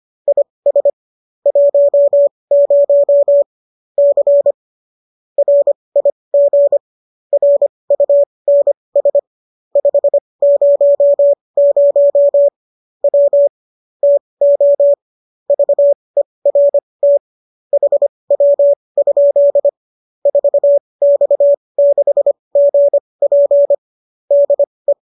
Morse code, heard as a keyed tone, about 25 wpm, spaced as for 20 wpm, 570 Hz. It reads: IS 10 C RIG RUNS 500 W TO VERT HW? 4X6GP DE